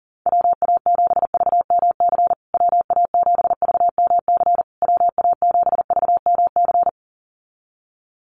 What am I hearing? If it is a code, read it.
Morse code, heard as a keyed tone, about 40 wpm, 700 Hz. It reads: WA7VMC WA7VMC WA7VMC
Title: USA Call Signs Repeated 3x 40wpm